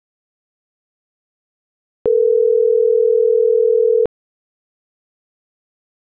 Ringback tone (North America), 440 + 480 Hz.
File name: CENTRAL TERRESTRE / out karis - .6135